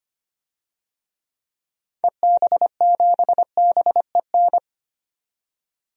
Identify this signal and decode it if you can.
Morse code, heard as a keyed tone, about 25 wpm, 700 Hz. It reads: EB7BEN
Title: DXCC-100 Call Signs - 6 Characters - 25wpm